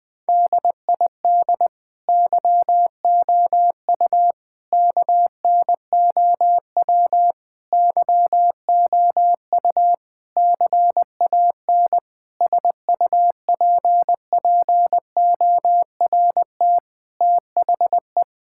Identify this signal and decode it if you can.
Morse code, heard as a keyed tone, about 19 wpm, 700 Hz. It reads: DID YOU KNOW YOU CAN SUPPORT THE
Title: Sentences with 9-Word Phrases - Spoken Beginning - 20wpm